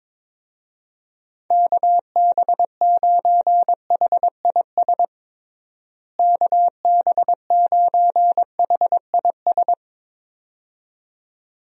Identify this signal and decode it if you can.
Morse code, heard as a keyed tone, about 22 wpm, 700 Hz. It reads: KB9HIS KB9HIS
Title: USA Call Signs - 6 Characters - Repeated 2x - 22wpm